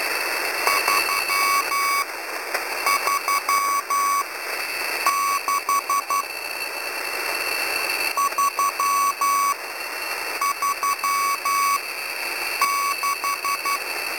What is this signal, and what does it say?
Morse code, heard as a keyed tone, about 12 wpm, spaced as for 7 wpm, 1120 Hz, with noise. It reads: 336 336